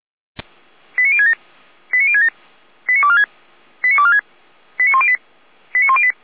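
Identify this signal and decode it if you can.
Five-tone selective call (ZVEI), heard as quick runs of five tones, tones of about 70 ms. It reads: 89067 89067 89246 89246 89108 89108
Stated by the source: ZVEI five-tone selective call received on an SDR.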